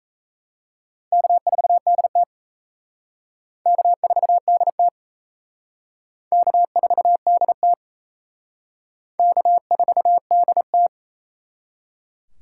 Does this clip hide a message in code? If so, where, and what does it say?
Yes: Morse code from 1.1 to 10.9 s, heard as a keyed tone, about 36 wpm, 700 Hz. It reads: K4DT K4DT K4DT K4DT